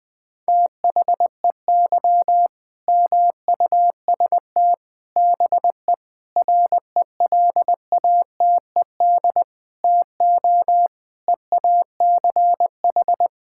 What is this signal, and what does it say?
Morse code, heard as a keyed tone, about 21 wpm, 700 Hz. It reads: THEY MUST BE RELATED TO EACH
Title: Sentences from Top 1000 Words - Review Part 5 - 20wpm